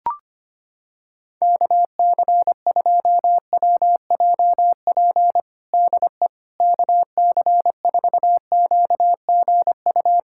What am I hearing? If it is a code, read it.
Morse code, heard as a keyed tone, about 25 wpm, 700 Hz. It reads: KC2WJP DE KC4QGU
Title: USA Callsign DE Callsign - 6 Characters - 25wpm